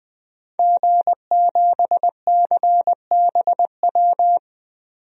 Morse code, heard as a keyed tone, about 20 wpm, 700 Hz. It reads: G7CBW
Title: DXCC-100 Call Signs - 5 Characters - 20wpm